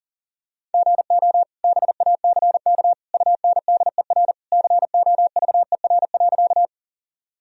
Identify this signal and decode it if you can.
Morse code, heard as a keyed tone, about 40 wpm, 700 Hz. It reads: GO BACK UNDER COVER.
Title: Sentences from Top 1000 Words - Review Part 1 - 40wpm